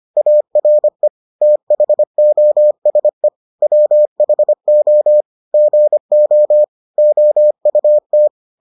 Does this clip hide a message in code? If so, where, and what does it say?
Yes: Morse code from 0.2 to 8.3 s, heard as a keyed tone, about 25 wpm, 600 Hz. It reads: ARE THOSE WHO GO OUT